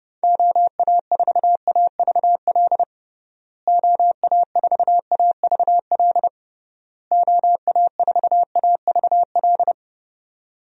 Morse code, heard as a keyed tone, about 30 wpm, 700 Hz. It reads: OA4AVL OA4AVL OA4AVL
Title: USA Call Signs - 6 Characters - Repeated 3x - 30wpm